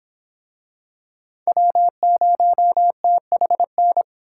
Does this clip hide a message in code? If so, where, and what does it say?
Yes: Morse code from 1.5 to 4.0 s, heard as a keyed tone, about 26 wpm, 700 Hz. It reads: W0THN